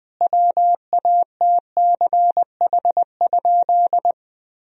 Morse code, heard as a keyed tone, about 20 wpm, 700 Hz. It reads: WATCH?
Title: Sentences from Top 1000 Words - Review Part 3 - 20wpm